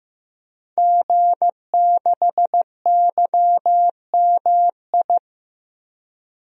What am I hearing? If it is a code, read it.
Morse code, heard as a keyed tone, about 15 wpm, 700 Hz. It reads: G6YMI
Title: DXCC-100 Call Signs - 5 Characters - Repeated 2x - 15wpm